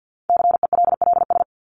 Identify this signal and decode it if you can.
Morse code, heard as a keyed tone, about 48 wpm, 700 Hz. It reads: CELLS
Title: ES:S Suffix - Top 100 Words with Base Words 50wpm